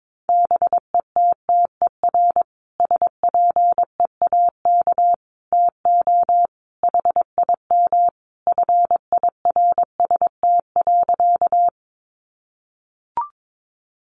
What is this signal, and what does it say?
Morse code, heard as a keyed tone, about 22 wpm, 700 Hz. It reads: BETTER SPEAK TO HIM FIRST.
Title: Sentences from Top 1100 Words 22wpm